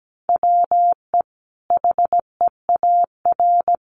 Morse code, heard as a keyed tone, about 17 wpm, 700 Hz. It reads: WE HEAR
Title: Sentences from Top 1400 Words 17wpm